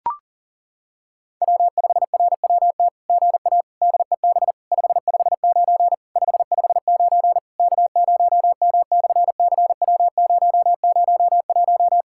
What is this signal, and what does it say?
Morse code, heard as a keyed tone, about 40 wpm, 700 Hz. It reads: W5RWT GA DEB 559 559 K0M/CW001
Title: SOTA QSO Practice 40wpm